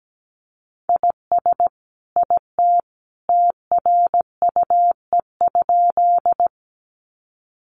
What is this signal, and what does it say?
Morse code, heard as a keyed tone, about 17 wpm, 700 Hz. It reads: IS IT TRUE?
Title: Sentences from Top 400 Words 17wpm